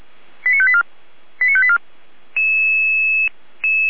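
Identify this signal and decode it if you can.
Five-tone selective call (ZVEI), heard as quick runs of five tones, tones of about 70 ms. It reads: 89584 89584
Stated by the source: Received on an SDR; ZVEI five-tone selective call.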